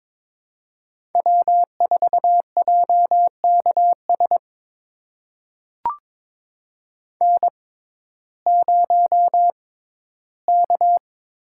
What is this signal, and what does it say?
Morse code, heard as a keyed tone, about 22 wpm, spaced as for 14 wpm, 700 Hz. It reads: W4JKS N0K
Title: USA Call Signs - 5 Characters - 6x Character Spacing - 22wpm